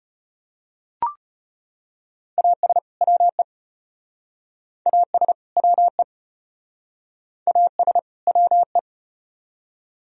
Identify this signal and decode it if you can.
Morse code, heard as a keyed tone, about 35 wpm, 700 Hz. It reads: AS WE AS WE AS WE